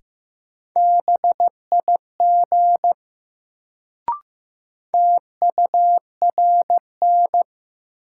Morse code, heard as a keyed tone, about 15 wpm, 700 Hz. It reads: BIG TURN
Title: Sentences from Top 200 Words - New Word Drill - 15wpm